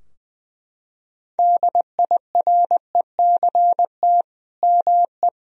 Morse code, heard as a keyed tone, about 20 wpm, 700 Hz. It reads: DIRECT ME
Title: Sentences from Top 1000 Words - Review Part 2 - 20wpm